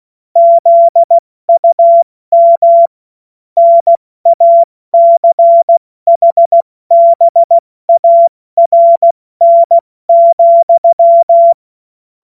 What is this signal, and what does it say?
Morse code, heard as a keyed tone, about 16 wpm, 680 Hz. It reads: ZUM NACHBARN,
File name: Witz-29-Juli-2025-70-F80